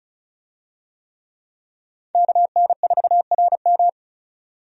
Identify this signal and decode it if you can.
Morse code, heard as a keyed tone, about 34 wpm, 700 Hz. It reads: KN4RM
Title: USA Call Signs - 5 Characters - 35wpm